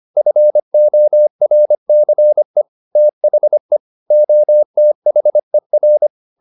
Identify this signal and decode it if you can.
Morse code, heard as a keyed tone, about 25 wpm, 600 Hz. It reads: FORCE THE OTHER
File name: the old man and the sea part 1 25wpm